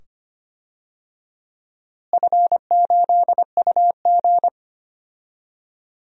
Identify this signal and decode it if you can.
Morse code, heard as a keyed tone, about 25 wpm, 700 Hz. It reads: F8UG
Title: DXCC-100 Call Signs - 4 Characters - Repeated 2x - 25wpm